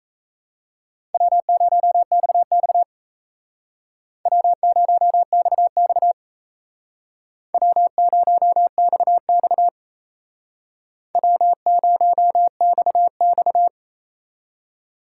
Morse code, heard as a keyed tone, about 36 wpm, 700 Hz. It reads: W0XX W0XX W0XX W0XX